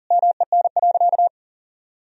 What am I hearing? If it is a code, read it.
Morse code, heard as a keyed tone, about 40 wpm, 700 Hz. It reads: MEN.
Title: Sentences from Top 1000 Words - Review Part 3 - 40wpm